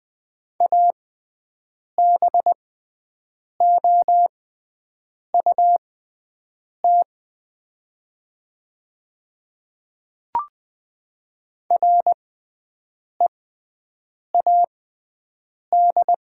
Morse code, heard as a keyed tone, about 20 wpm, spaced as for 7 wpm, 700 Hz. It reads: ABOUT READ
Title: 6x Character Spacing - Top 200 Words - 20wpm